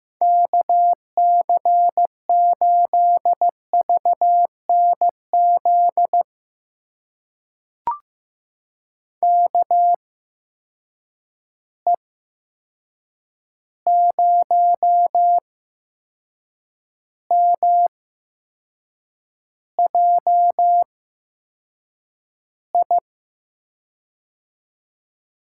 Morse code, heard as a keyed tone, about 15 wpm, spaced as for 6 wpm, 700 Hz. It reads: KC8VNZ KE0MJI
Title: USA Call Signs - 6 Characters - 8x Character Spacing - 15wpm